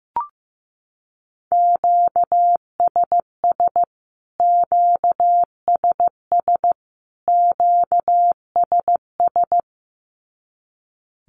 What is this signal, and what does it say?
Morse code, heard as a keyed tone, about 15 wpm, 700 Hz. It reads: QSS QSS QSS
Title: Q-Codes Repeated 3x 15wpm